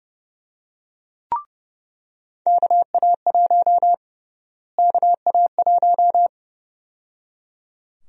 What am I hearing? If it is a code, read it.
Morse code, heard as a keyed tone, about 31 wpm, 700 Hz. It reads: KA1 KA1
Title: USA Call Sign Prefixes - Repeated 2x - 30wpm